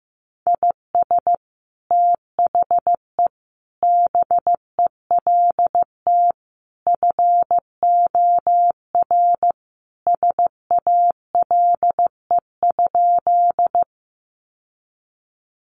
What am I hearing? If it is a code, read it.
Morse code, heard as a keyed tone, about 14 wpm, 700 Hz. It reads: IS THE BELT FOR SALE?
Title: Sentences from Top 1400 Words 15wpm